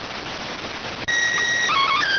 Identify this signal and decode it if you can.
SSTV (slow-scan television), PD 290.